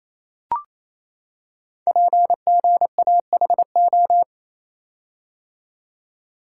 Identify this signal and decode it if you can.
Morse code, heard as a keyed tone, about 28 wpm, 700 Hz. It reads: PGAHO
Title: Code Groups - 5 Letters - 28wpm